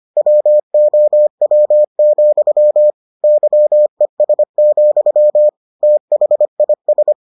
Morse code, heard as a keyed tone, about 25 wpm, 600 Hz. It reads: WOW, YES, THIS